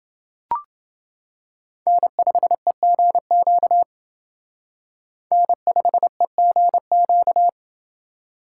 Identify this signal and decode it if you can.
Morse code, heard as a keyed tone, about 29 wpm, 700 Hz. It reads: N5EGQ N5EGQ